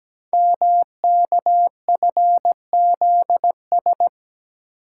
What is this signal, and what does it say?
Morse code, heard as a keyed tone, about 17 wpm, 700 Hz. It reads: MKFZS
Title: Code Groups - 5 Letters - 17wpm